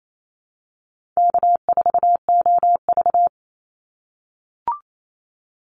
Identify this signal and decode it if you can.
Morse code, heard as a keyed tone, about 28 wpm, 700 Hz. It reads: K4OV
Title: Easy Call Signs - 28wpm